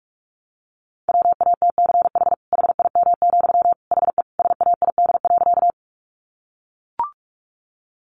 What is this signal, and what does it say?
Morse code, heard as a keyed tone, about 44 wpm, 700 Hz. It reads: WATCH HIM, HE SAID.
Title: Sentences from Top 300 Words 45wpm